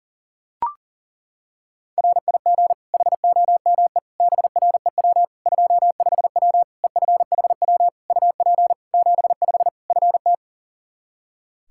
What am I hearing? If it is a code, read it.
Morse code, heard as a keyed tone, about 37 wpm, 700 Hz. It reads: RIG HOME BREW 25W EFHW UP 75 FT
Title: K1USN SST QSO Practice 40wpm